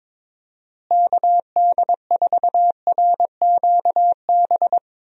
Morse code, heard as a keyed tone, about 22 wpm, 700 Hz. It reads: KD4RQB
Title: USA Call Signs - 6 Characters - Repeated 3x - 22wpm